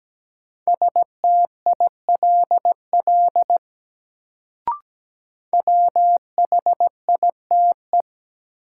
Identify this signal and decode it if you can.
Morse code, heard as a keyed tone, about 17 wpm, 700 Hz. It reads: STILL WHITE
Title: Sentences from Top 300 Words - New Word Drill - 17wpm